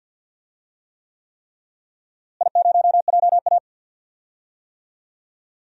Morse code, heard as a keyed tone, about 50 wpm, 700 Hz. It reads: I0JA
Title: DXCC-100 Call Signs - 4 Characters - Repeated 3x - 50wpm